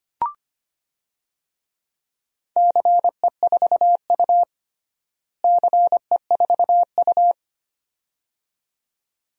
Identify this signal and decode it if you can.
Morse code, heard as a keyed tone, about 25 wpm, 700 Hz. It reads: CE4U CE4U
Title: DXCC-100 Call Signs - 4 Characters - Repeated 2x - 25wpm